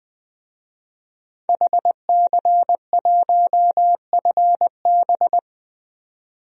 Morse code, heard as a keyed tone, about 20 wpm, 700 Hz. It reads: HC1FB
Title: DXCC-100 Call Signs - 5 Characters - 20wpm